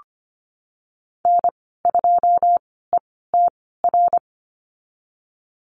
Morse code, heard as a keyed tone, about 25 wpm, spaced as for 16 wpm, 700 Hz. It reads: N2ETR
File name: Lesson 15g - Number 5 - Call Signs 25wpm